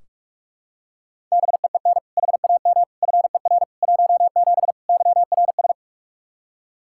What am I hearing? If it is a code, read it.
Morse code, heard as a keyed tone, about 46 wpm, 700 Hz. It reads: BEEN HAM FER 17 YRS
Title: Practice QSO - Exchange 3 - More Info - 45wpm